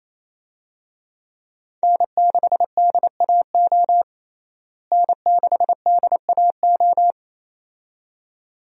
Morse code, heard as a keyed tone, about 28 wpm, 700 Hz. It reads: N6DAO N6DAO
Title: USA Call Signs - 5 Characters - Repeated 2x - 28wpm